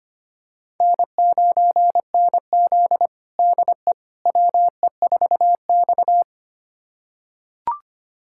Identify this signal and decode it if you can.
Morse code, heard as a keyed tone, about 25 wpm, 700 Hz. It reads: N9NZ DE WE4X
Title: USA Callsign DE Callsign - 4 Characters - 25wpm